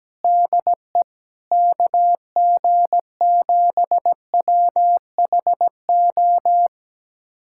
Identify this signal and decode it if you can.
Morse code, heard as a keyed tone, about 17 wpm, 700 Hz. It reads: DE KG7WHO
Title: USA Callsign DE Callsign 17wpm